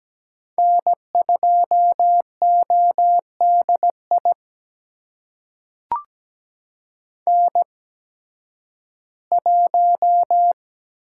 Morse code, heard as a keyed tone, about 17 wpm, spaced as for 11 wpm, 700 Hz. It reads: N2ODI N1
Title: USA Call Signs - 5 Characters - 8x Character Spacing - 17wpm